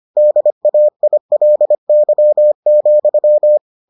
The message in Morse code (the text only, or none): DAILY,